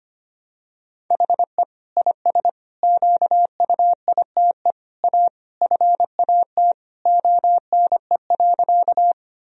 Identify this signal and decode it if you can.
Morse code, heard as a keyed tone, about 25 wpm, 700 Hz. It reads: HE IS QUITE A FAT ONE.
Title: Sentences from Top 1000 Words - Review Part 4 - 25wpm